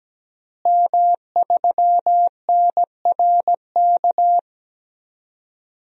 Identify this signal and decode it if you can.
Morse code, heard as a keyed tone, about 17 wpm, 700 Hz. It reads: M3NRK
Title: DXCC-100 Call Signs - 5 Characters - Repeated 3x - 17wpm